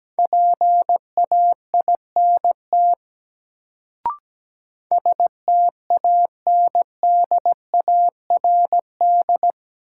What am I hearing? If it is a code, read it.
Morse code, heard as a keyed tone, about 17 wpm, 700 Hz. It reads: PAINT STANDARD
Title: Sentences from Top 1000 Words - Review Part 6 - New Word Drill - 17wpm